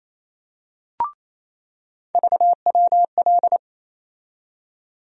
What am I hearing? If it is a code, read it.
Morse code, heard as a keyed tone, about 28 wpm, 700 Hz. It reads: VWL
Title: USA Call Sign Postfixes - 3 Letters 28wpm